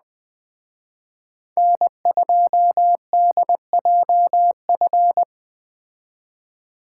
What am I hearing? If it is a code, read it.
Morse code, heard as a keyed tone, about 21 wpm, 700 Hz. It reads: N2DJF